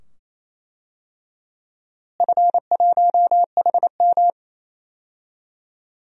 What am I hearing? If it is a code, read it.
Morse code, heard as a keyed tone, about 28 wpm, 700 Hz. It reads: F1HM
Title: DXCC-100 Call Signs - 4 Characters - Repeated 2x - 28wpm